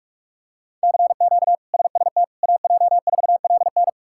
Morse code, heard as a keyed tone, about 46 wpm, 700 Hz. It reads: CQ SST AJ4LN
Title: K1USN SST QSO Practice - 45wpm